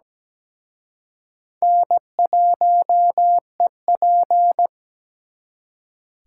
Morse code, heard as a keyed tone, about 20 wpm, 700 Hz. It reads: N1EP
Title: Speed Racing - USA 1x2 Call Signs 17wpm